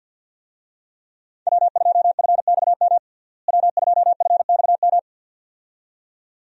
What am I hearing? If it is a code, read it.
Morse code, heard as a keyed tone, about 50 wpm, 700 Hz. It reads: W2FXM W2FXM
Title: USA Call Signs - 5 Characters - Repeated 2x - 50wpm